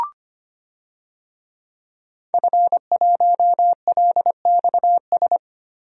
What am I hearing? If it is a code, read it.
Morse code, heard as a keyed tone, about 25 wpm, 700 Hz. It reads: F1LXS
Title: DXCC-100 Call Signs - 5 Characters - Repeated 3x - 25wpm